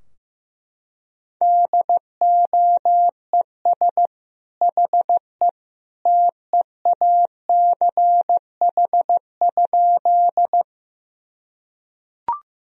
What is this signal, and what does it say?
Morse code, heard as a keyed tone, about 15 wpm, 700 Hz. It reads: DOES HE TEACH?
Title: Sentences from Top 1000 Words - Review Part 2 - 15wpm